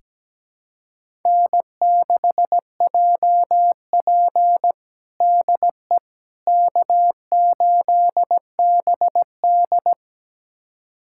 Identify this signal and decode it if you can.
Morse code, heard as a keyed tone, about 17 wpm, 700 Hz. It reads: N6JP DE K8BD
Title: USA Callsign DE Callsign - 4 Characters - 17wpm